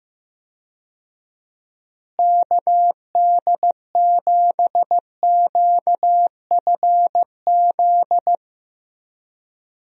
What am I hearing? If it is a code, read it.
Morse code, heard as a keyed tone, about 16 wpm, 700 Hz. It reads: KD7QFZ
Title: USA Call Signs - 6 Characters - Repeated 3x - 15wpm